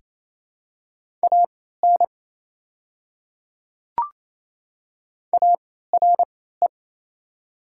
Morse code, heard as a keyed tone, about 28 wpm, spaced as for 16 wpm, 690 Hz. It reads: AN ARE
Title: 3x Character Spacing - Top 100 Words - 28wpm